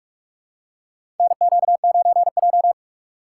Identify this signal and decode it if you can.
Morse code, heard as a keyed tone, about 45 wpm, 700 Hz. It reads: NQ9J
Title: USA Call Signs - 4 Characters - 6x Character Spacing - 45wpm